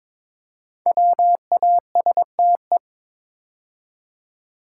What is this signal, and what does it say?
Morse code, heard as a keyed tone, about 22 wpm, 700 Hz. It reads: WASTE
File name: 5 Letter Words - First 3 Letters Given - 22wpm